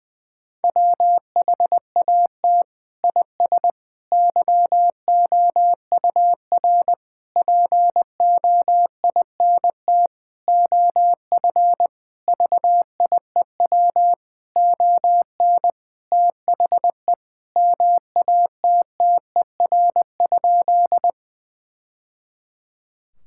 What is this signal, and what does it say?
Morse code, heard as a keyed tone, about 20 wpm, 700 Hz. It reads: WHAT IS YOUR POINT OF VIEW ON THE MATTER?
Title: Sentences from Top 1000 Words - Review Part 6 - 20wpm